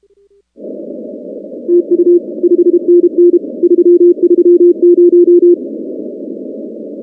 Morse code, heard as a keyed tone, about 32 wpm, 350 Hz. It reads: TU 5NN 330
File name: contests / 2019-cw-wpx-kt5j-k5pi- -2330 0-19